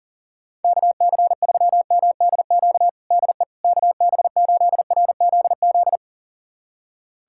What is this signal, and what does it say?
Morse code, heard as a keyed tone, about 40 wpm, 700 Hz. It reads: KC3MDQ DE KB8RZZ
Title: USA Callsign DE Callsign 40wpm